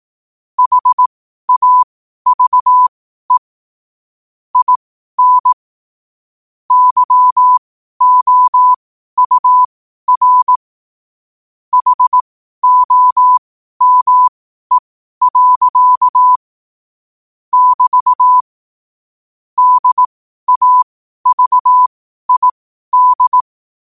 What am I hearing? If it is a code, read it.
Morse code, heard as a keyed tone, about 18 wpm, spaced as for 12 wpm, 1000 Hz. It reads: HAVE IN YOUR HOME. <BT> DAVID
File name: qotdfiles / 12wpm